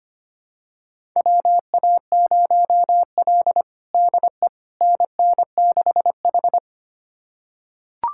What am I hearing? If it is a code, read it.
Morse code, heard as a keyed tone, about 25 wpm, 700 Hz. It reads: WA0L DE NN6H
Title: USA Callsign DE Callsign - 4 Characters - 25wpm